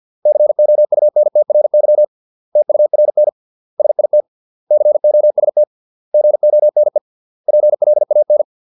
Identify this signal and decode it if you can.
Morse code, heard as a keyed tone, about 50 wpm, 600 Hz. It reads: COUNTRY TURN SIT COST GONE PLAN